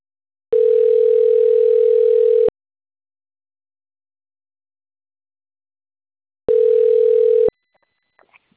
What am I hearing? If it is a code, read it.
Ringback tone (North America), 440 + 480 Hz.